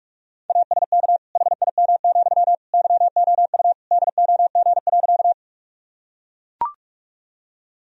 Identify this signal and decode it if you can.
Morse code, heard as a keyed tone, about 44 wpm, 700 Hz. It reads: ASK HIM, YOU DOG.
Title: Sentences from Top 1000 Words - Review Part 2 - 45wpm